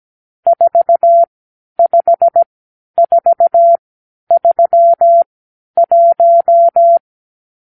Morse code, heard as a keyed tone, about 17 wpm, spaced as for 10 wpm, 680 Hz. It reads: 45431
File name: k7qo / Lesson 52 (8)